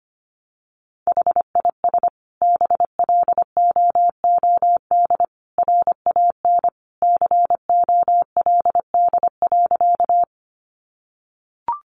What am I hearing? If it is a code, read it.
Morse code, heard as a keyed tone, about 25 wpm, 700 Hz. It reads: HIS BLOOD RAN COLD.
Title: Sentences from Top 1100 Words 25wpm